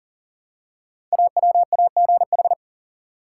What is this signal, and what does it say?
Morse code, heard as a keyed tone, about 41 wpm, 700 Hz. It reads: AWAGH
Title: Code Groups - 5 Letter:Numbers - 40wpm